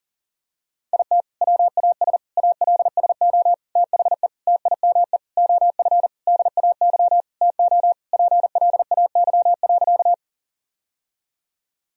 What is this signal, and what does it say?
Morse code, heard as a keyed tone, about 40 wpm, 700 Hz. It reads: IT WAS ALSO THE TIME OF DAY TO PLAY.
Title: Sentences from Top 1000 Words - Review Part 1 - 40wpm